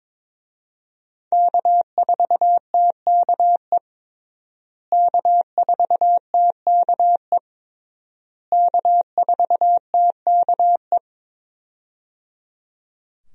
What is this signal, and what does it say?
Morse code, heard as a keyed tone, about 22 wpm, 700 Hz. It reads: K4TKE K4TKE K4TKE
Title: USA Call Signs - 5 Characters - Repeated 3x - 22wpm